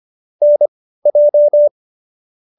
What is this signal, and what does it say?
Morse code, heard as a keyed tone, about 25 wpm, spaced as for 15 wpm, 590 Hz. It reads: NJ